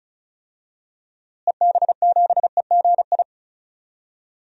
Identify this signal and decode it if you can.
Morse code, heard as a keyed tone, about 36 wpm, 700 Hz. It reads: EB7EGI
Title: DXCC-100 Call Signs - 6 Characters - Repeated 3x - 35wpm